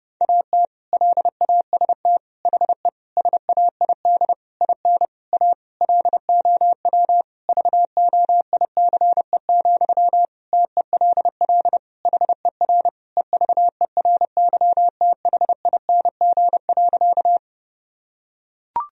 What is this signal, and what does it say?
Morse code, heard as a keyed tone, about 30 wpm, 700 Hz. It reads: AT LAST HE SAID IN A LOW VOICE, TELL HER EVERYTHING.
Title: Sentences from Top 1000 Words - Review Part 1 - 30wpm